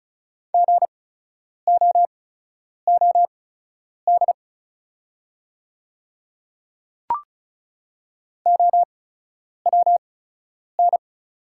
Morse code, heard as a keyed tone, about 36 wpm, spaced as for 10 wpm, 690 Hz. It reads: GOOD OWN
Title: 8x Character Spacing - Top 300 Words - 35wpm